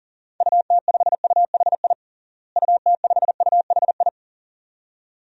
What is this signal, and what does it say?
Morse code, heard as a keyed tone, about 40 wpm, 700 Hz. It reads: UT5UHI UT5UHI
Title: DXCC-100 Call Signs - 6 Characters - Repeated 2x - 40wpm